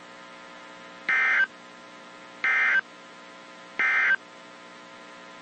Emergency Alert System (SAME) tones, heard as harsh buzzing bursts. It is heard from 1.0 to 4.2 s.
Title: WZ2523 - NOAA Weather Radio